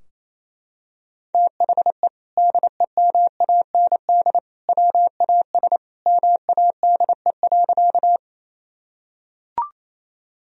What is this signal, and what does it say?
Morse code, heard as a keyed tone, about 28 wpm, 700 Hz. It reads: THE DEMAND WAS MADE.
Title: Sentences from Top 1000 Words - Review Part 4 - 28wpm